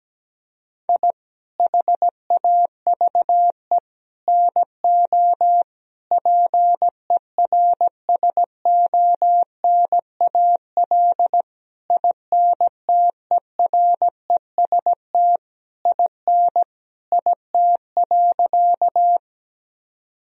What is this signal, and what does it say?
Morse code, heard as a keyed tone, about 17 wpm, 700 Hz. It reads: I HAVE NO PERSONAL INTEREST IN IT.
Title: Sentences from Top 1000 Words - Review Part 4 - 17wpm